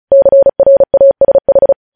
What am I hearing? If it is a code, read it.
Morse code, heard as a keyed tone, about 36 wpm, 550 Hz. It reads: CRASH
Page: news headlines in morse code at 35 wpm